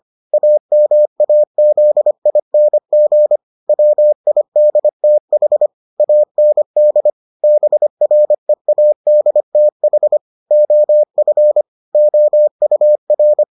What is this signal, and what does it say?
Morse code, heard as a keyed tone, about 25 wpm, 600 Hz. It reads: AMAZING WIDTH AND BREADTH OF OUR